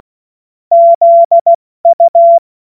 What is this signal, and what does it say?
Morse code, heard as a keyed tone, about 16 wpm, spaced as for 14 wpm, 680 Hz. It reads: ZU
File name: Morsekurs / Witze / Witz 13 Februar 2024 70 F80